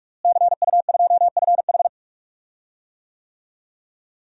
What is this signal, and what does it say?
Morse code, heard as a keyed tone, about 46 wpm, 700 Hz. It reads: CU2FH
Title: DXCC-100 Call Signs - 5 Characters - Repeated 3x - 45wpm